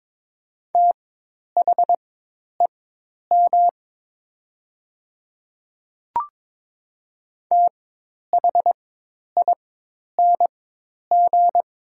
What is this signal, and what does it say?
Morse code, heard as a keyed tone, about 22 wpm, spaced as for 10 wpm, 700 Hz. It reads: THEM THING
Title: 4x Character Spacing - Top 100 Words - 22wpm